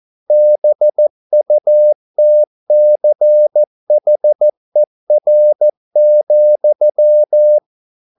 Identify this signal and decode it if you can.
Morse code, heard as a keyed tone, about 14 wpm, 600 Hz. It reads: BUTCHER,
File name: mp3 files / a dogs mistake - 14wpm